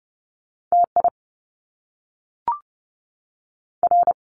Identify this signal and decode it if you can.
Morse code, heard as a keyed tone, about 30 wpm, 700 Hz. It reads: TI R
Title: Top Two Letter Combinations in Words 30wpm